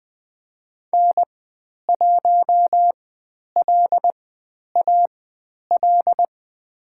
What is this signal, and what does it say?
Morse code, heard as a keyed tone, about 21 wpm, spaced as for 10 wpm, 700 Hz. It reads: N1LAL
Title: 109 Lesson 18g - Letter m - Call Signs 20wpm